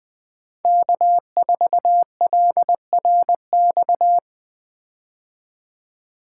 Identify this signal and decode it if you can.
Morse code, heard as a keyed tone, about 20 wpm, 700 Hz. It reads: K4LRX
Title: K1USN SST QSO Practice - 20wpm